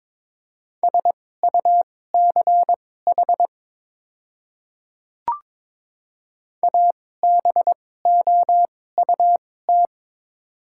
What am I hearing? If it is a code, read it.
Morse code, heard as a keyed tone, about 22 wpm, spaced as for 16 wpm, 700 Hz. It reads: SUCH ABOUT
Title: 2x Character Spacing - Top 200 Words - 22wpm